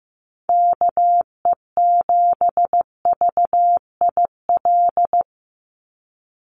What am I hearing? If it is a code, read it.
Morse code, heard as a keyed tone, about 15 wpm, 700 Hz. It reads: KE7VIL
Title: USA Call Signs 15wpm